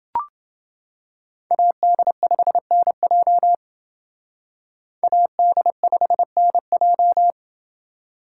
Morse code, heard as a keyed tone, about 31 wpm, 700 Hz. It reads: AD5NJ AD5NJ